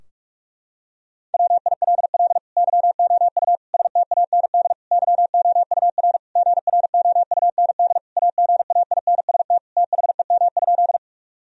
Morse code, heard as a keyed tone, about 43 wpm, 700 Hz. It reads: WILL YOU STAND YOUR GROUND AGAINST THEM?
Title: Sentences from Top 1000 Words - Review Part 2 - 45wpm